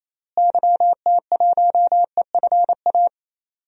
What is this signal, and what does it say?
Morse code, heard as a keyed tone, about 28 wpm, 700 Hz. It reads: YT1EFA
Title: DXCC-100 Call Signs - 6 Characters - Repeated 2x - 28wpm